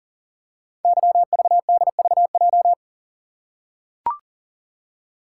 Morse code, heard as a keyed tone, about 41 wpm, 690 Hz. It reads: YVDVJ
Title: Code Groups - 5 Letters - 40wpm